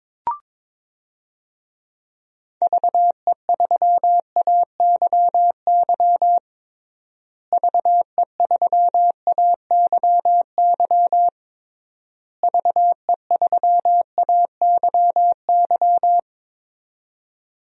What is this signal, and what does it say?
Morse code, heard as a keyed tone, about 22 wpm, 700 Hz. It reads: VE3AYY VE3AYY VE3AYY
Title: DXCC-100 Call Signs - 6 Characters - Repeated 3x - 22wpm